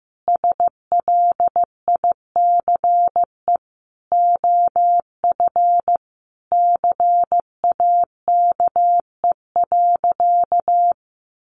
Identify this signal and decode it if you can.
Morse code, heard as a keyed tone, about 15 wpm, 700 Hz. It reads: SLICE OF CAKE.
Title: Additional Easy Phrases 15wpm